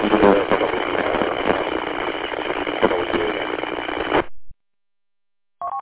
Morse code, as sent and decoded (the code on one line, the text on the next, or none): none